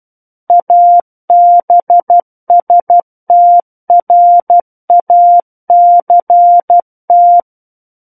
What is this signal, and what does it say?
Morse code, heard as a keyed tone, about 12 wpm, 700 Hz. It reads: ABSTRACT
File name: Book One / 03-Chapter 3 - 12 WPM